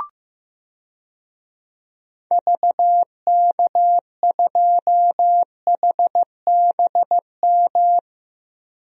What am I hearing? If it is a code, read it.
Morse code, heard as a keyed tone, about 15 wpm, 700 Hz. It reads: VK2HBM
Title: DXCC-100 Call Signs - 6 Characters - Repeated 3x - 15wpm